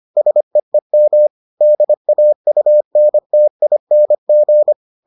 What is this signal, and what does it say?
Morse code, heard as a keyed tone, about 25 wpm, 600 Hz. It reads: SEEM DAUNTING